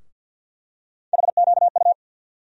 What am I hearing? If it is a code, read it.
Morse code, heard as a keyed tone, about 50 wpm, 700 Hz. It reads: HQU
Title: USA Call Sign Postfixes - 3 Letters 50wpm